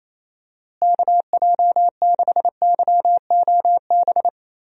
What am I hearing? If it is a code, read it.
Morse code, heard as a keyed tone, about 28 wpm, 700 Hz. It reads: KJ6YOB
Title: USA Call Signs - 6 Characters - Repeated 2x - 28wpm